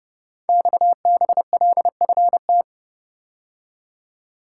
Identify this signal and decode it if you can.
Morse code, heard as a keyed tone, about 30 wpm, 700 Hz. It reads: XBLFT
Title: Code Groups - 5 Letters - 30wpm